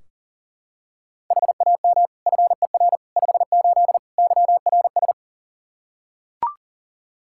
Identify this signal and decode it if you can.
Morse code, heard as a keyed tone, about 40 wpm, 700 Hz. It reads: HAM FER 58 YRS
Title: Practice QSO - Exchange 3 - More Info - 40wpm